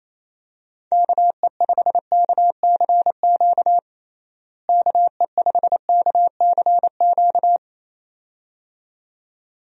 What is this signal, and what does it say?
Morse code, heard as a keyed tone, about 28 wpm, 700 Hz. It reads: KE5KCQ KE5KCQ
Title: USA Call Signs - 6 Characters - Repeated 2x - 28wpm